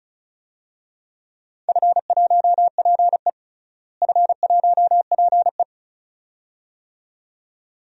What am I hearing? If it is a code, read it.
Morse code, heard as a keyed tone, about 34 wpm, 700 Hz. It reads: F1PE F1PE
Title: DXCC-100 Call Signs - 4 Characters - Repeated 2x - 35wpm